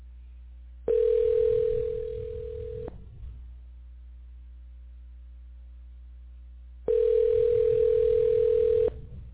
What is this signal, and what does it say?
Ringback tone (North America), 440 + 480 Hz.